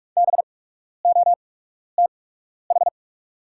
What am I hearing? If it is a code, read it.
Morse code, heard as a keyed tone, about 48 wpm, spaced as for 12 wpm, 690 Hz. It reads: BOTH
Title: 8x Character Spacing - Top 500 Words - 45wpm